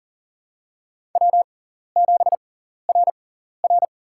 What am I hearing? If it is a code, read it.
Morse code, heard as a keyed tone, about 40 wpm, spaced as for 14 wpm, 700 Hz. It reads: W7RR
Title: USA Call Signs - 4 Characters - 6x Character Spacing - 40wpm